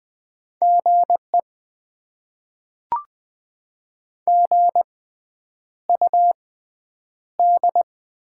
Morse code, heard as a keyed tone, about 20 wpm, spaced as for 9 wpm, 690 Hz. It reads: GE GUD
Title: Top 20 QSO Elements - 6x Character Spacing - 20wpm